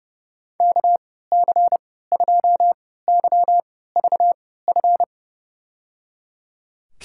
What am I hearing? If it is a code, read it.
Morse code, heard as a keyed tone, about 30 wpm, spaced as for 17 wpm, 700 Hz. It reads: KC2YVF
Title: USA Call Signs - 6 Characters - 3x Character Spacing - 30wpm